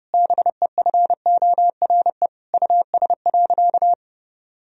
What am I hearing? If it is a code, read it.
Morse code, heard as a keyed tone, about 32 wpm, 700 Hz. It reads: BEFORE US.
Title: Sentences from Top 1000 Words - Review Part 1 - 30wpm